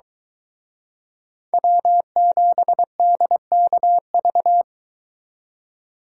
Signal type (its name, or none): Morse code